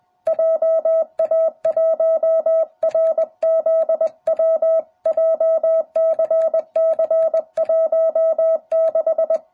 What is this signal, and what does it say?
Morse code, heard as a keyed tone, about 20 wpm, 660 Hz. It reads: JA1RZW JCC16